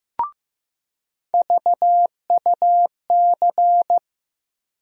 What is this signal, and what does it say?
Morse code, heard as a keyed tone, about 15 wpm, 700 Hz. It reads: VUC
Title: USA Call Sign Postfixes - 3 Letters 15wpm